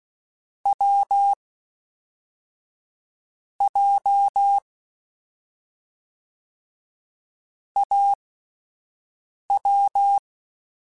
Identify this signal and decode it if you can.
Morse code, heard as a keyed tone, about 16 wpm, 800 Hz. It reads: W J A W